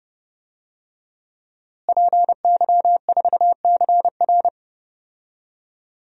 Morse code, heard as a keyed tone, about 31 wpm, 700 Hz. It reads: PY4CR
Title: DXCC-100 Call Signs - 5 Characters - Repeated 2x - 30wpm